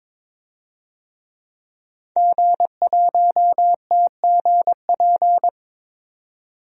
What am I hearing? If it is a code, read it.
Morse code, heard as a keyed tone, about 22 wpm, 700 Hz. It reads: G1TGP
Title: DXCC-100 Call Signs - 5 Characters - 22wpm